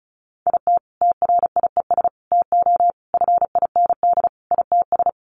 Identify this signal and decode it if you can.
Morse code, heard as a keyed tone, about 34 wpm, 700 Hz. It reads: IT TRIES TO FIND ITS
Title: Additional Easy Phrases 35wpm